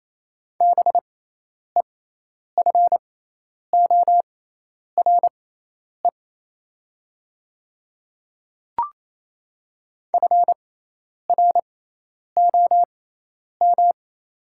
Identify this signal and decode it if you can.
Morse code, heard as a keyed tone, about 28 wpm, spaced as for 10 wpm, 700 Hz. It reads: BEFORE FROM
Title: 6x Character Spacing - Top 300 Words - 28wpm